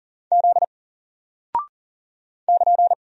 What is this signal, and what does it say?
Morse code, heard as a keyed tone, about 40 wpm, 700 Hz. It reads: Z <KN>
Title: Single Character (Letter:Number:Punctuation:Prosign) Alt 40wpm